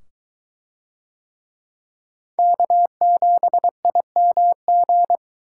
Morse code, heard as a keyed tone, about 23 wpm, 700 Hz. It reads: K7IMG